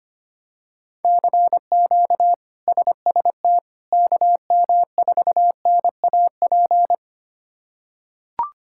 Morse code, heard as a keyed tone, about 25 wpm, 700 Hz. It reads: CQ SST KM4NAP
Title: K1USN SST QSO Practice - 25wpm